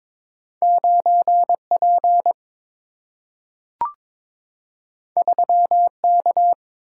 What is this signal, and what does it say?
Morse code, heard as a keyed tone, about 22 wpm, 700 Hz. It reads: 9 P 3 K
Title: Single-Letter Single-Number 22wpm